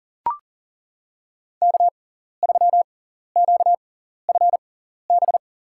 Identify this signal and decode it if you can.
Morse code, heard as a keyed tone, about 40 wpm, spaced as for 14 wpm, 700 Hz. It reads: K3QFB
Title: USA Call Signs - 5 Characters - 6x Character Spacing - 40wpm